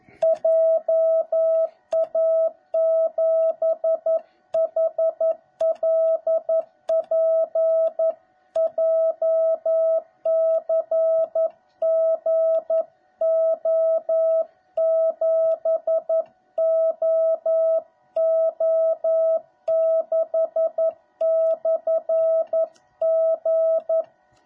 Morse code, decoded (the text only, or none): JA7HLP JCGO7OO6/G